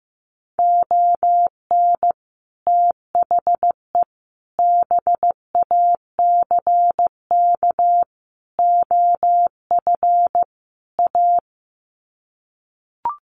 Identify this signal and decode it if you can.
Morse code, heard as a keyed tone, about 15 wpm, 700 Hz. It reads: ON THE BACK OF A
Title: Sets of 5 Words Encore2 15wpm